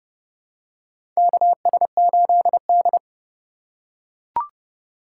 Morse code, heard as a keyed tone, about 30 wpm, 700 Hz. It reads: KS8D
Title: USA Call Signs - 4 Characters - 4x Character Spacing - 30wpm